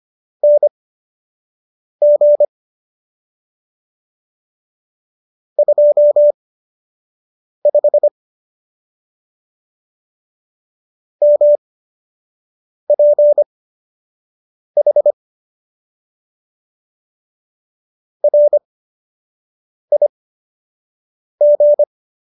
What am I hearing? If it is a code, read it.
Morse code, heard as a keyed tone, about 25 wpm, spaced as for 6 wpm, 590 Hz. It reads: NG 25 MPH RIG